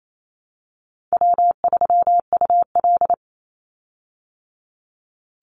Morse code, heard as a keyed tone, about 28 wpm, 700 Hz. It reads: W3UL
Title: Easy Call Signs Repeated 2x - 28wpm